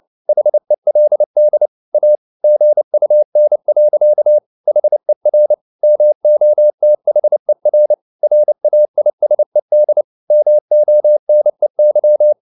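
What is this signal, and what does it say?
Morse code, heard as a keyed tone, about 29 wpm, 600 Hz. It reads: HELD A GUN. HER MOTHER RAISED MONEY
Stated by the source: Morse code at 29 wpm, sending Ken Follett's novel Never in English, from CW med Gnister.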